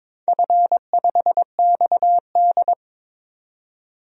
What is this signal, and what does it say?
Morse code, heard as a keyed tone, about 22 wpm, 700 Hz. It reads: F5XD
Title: DXCC-100 Call Signs - 4 Characters - 22wpm